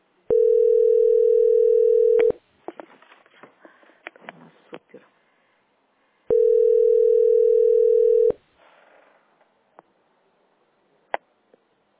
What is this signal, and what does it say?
Ringback tone (North America), 440 + 480 Hz.